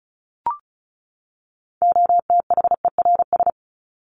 Morse code, heard as a keyed tone, about 34 wpm, 700 Hz. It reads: OTHERS
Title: ES:S Suffix - Top 100 Words with Base Words 35wpm